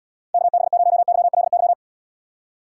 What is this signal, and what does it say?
Morse code, heard as a keyed tone, about 79 wpm, 700 Hz. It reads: KB0YLY